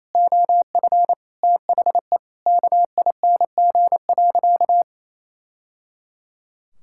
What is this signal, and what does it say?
Morse code, heard as a keyed tone, about 27 wpm, 700 Hz. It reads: OF THE KING.
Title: Sentences from Top 1000 Words - Review Part 5 - 28wpm